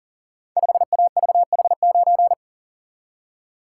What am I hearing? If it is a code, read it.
Morse code, heard as a keyed tone, about 39 wpm, 700 Hz. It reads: 5AVH9
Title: Code Groups - 5 Letter:Numbers - 40wpm